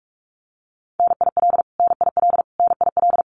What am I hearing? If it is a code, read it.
Morse code, heard as a keyed tone, about 44 wpm, 700 Hz. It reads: NIL NIL NIL
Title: Abbreviations Repeated 3x - 45wpm